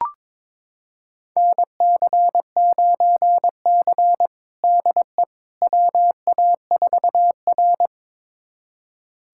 Morse code, heard as a keyed tone, about 22 wpm, 700 Hz. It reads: NC9C DE WA4R
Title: USA Callsign DE Callsign - 4 Characters - 22wpm